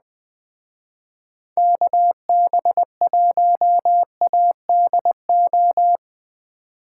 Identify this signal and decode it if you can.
Morse code, heard as a keyed tone, about 20 wpm, 700 Hz. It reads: KB1ADO
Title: Speed Racing - USA 2x3 Call Signs 17wpm